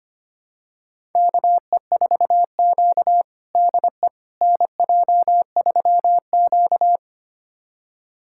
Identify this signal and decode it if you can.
Morse code, heard as a keyed tone, about 25 wpm, 700 Hz. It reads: KE4Q DE NJ3Q
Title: USA Callsign DE Callsign - 4 Characters - 25wpm